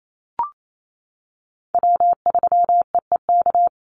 Morse code, heard as a keyed tone, about 28 wpm, 700 Hz. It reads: W3EEK
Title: Call Signs - 28wpm